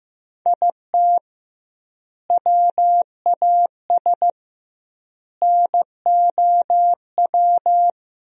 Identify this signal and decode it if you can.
Morse code, heard as a keyed tone, about 16 wpm, 700 Hz. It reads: IT WAS NOW
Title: 2x Word-Spacing - Sentences from Top 200 Words - 15wpm